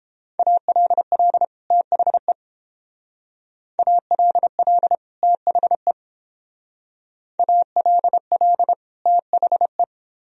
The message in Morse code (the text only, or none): ALL THE ALL THE ALL THE